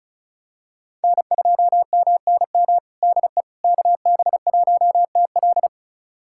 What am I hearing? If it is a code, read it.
Morse code, heard as a keyed tone, about 36 wpm, 700 Hz. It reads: N2MNM DE KB1TL
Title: USA Callsign DE Callsign - 5 Characters - 35wpm